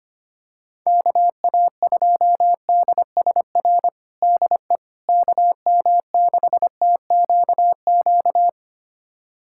Morse code, heard as a keyed tone, about 25 wpm, 700 Hz. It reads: KA2DSR DE KM6TQQ
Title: USA Callsign DE Callsign 25wpm